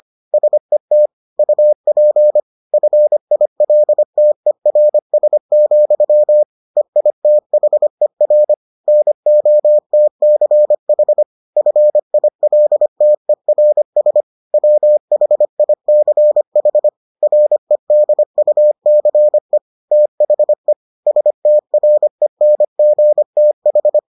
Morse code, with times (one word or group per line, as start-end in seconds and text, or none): 0.3-1.1 SET
1.4-2.4 UP
2.7-6.4 FILTERS,
6.8-8.5 EITHER
8.9-11.2 NOTCH
11.6-14.2 FILTERS
14.5-16.9 WHICH
17.2-19.6 REDUCE
19.9-20.7 THE
21.1-24.0 STRENGTH